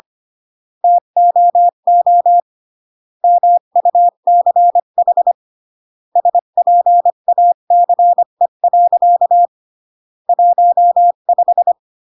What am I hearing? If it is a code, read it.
Morse code, heard as a keyed tone, about 25 wpm, 700 Hz. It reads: TOO MUCH SPACE. 15
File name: CW / jodorowsky / 82 commandments 25-20